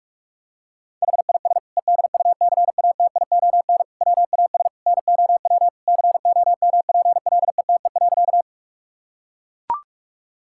Morse code, heard as a keyed tone, about 42 wpm, 700 Hz. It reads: HIS EDUCATION WAS NOW COMPLETE.
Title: Sentences from Top 1000 Words - Review Part 6 - 45wpm